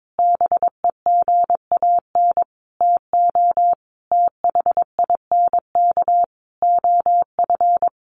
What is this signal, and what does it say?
Morse code, heard as a keyed tone, about 22 wpm, 700 Hz. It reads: BEGAN TO THINK OF
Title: Sentences from Top 800 Words 22wpm